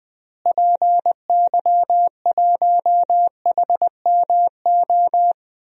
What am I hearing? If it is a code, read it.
Morse code, heard as a keyed tone, about 20 wpm, 700 Hz. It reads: PY1HMO
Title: DXCC-100 Call Signs - 6 Characters - Repeated 2x - 20wpm